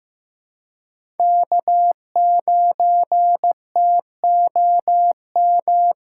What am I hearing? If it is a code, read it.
Morse code, heard as a keyed tone, about 15 wpm, 700 Hz. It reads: K9TOM
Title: USA Callsign DE Callsign 15wpm